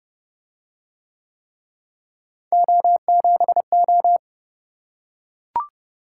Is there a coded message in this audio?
Yes: Morse code from 2.5 to 4.2 s, heard as a keyed tone, about 27 wpm, 700 Hz. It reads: O7O